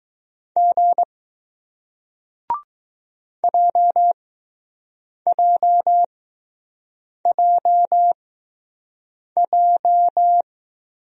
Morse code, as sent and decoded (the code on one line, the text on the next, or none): --. / .--- .--- .--- .---
G JJJJ